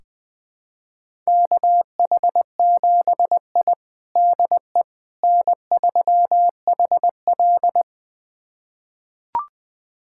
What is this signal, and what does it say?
Morse code, heard as a keyed tone, about 20 wpm, 700 Hz. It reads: KH7I DE N3HL
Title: USA Callsign DE Callsign - 4 Characters - 20wpm